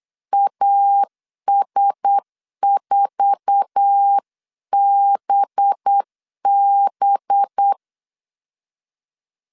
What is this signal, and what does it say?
Morse code, heard as a keyed tone, about 8 wpm, 800 Hz. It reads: AS4BB
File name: CW / DET / 1 9 1500C S B 8wpm 5624